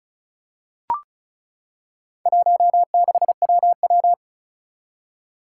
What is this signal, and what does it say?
Morse code, heard as a keyed tone, about 36 wpm, 690 Hz. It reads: 16WW
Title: Four Challenging Letter-Number Combinations 35wpm